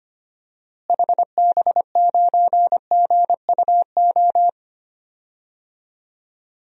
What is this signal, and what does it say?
Morse code, heard as a keyed tone, about 25 wpm, 700 Hz. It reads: HB9GUO
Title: DXCC-100 Call Signs - 6 Characters - Repeated 3x - 25wpm